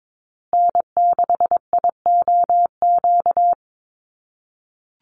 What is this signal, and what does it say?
Morse code, heard as a keyed tone, about 22 wpm, 700 Hz. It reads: N6IOQ
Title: USA Call Signs Repeated 3x 22wpm